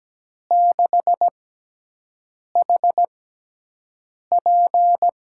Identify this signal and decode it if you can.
Morse code, heard as a keyed tone, about 17 wpm, spaced as for 6 wpm, 700 Hz. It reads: 6HP
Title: USA Call Signs - 5 Characters - 6x Character Spacing - 17wpm